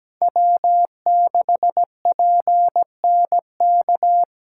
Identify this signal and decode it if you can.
Morse code, heard as a keyed tone, about 17 wpm, 700 Hz. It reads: W6PNK
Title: USA Call Signs - 3x Character Spacing - 17wpm